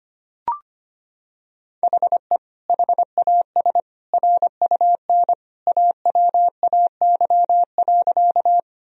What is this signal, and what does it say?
Morse code, heard as a keyed tone, about 25 wpm, 700 Hz. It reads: HE HAS RUN AWAY.
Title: Sentences from Top 1000 Words - Review Part 1 - 25wpm